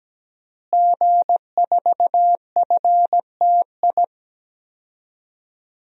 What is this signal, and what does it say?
Morse code, heard as a keyed tone, about 17 wpm, 700 Hz. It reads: G4FTI
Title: DXCC-100 Call Signs - 5 Characters - Repeated 3x - 17wpm